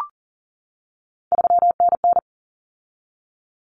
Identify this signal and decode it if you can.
Morse code, heard as a keyed tone, about 40 wpm, 700 Hz. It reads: 3NN
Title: Common Cut Numbers - 3-Digit 40wpm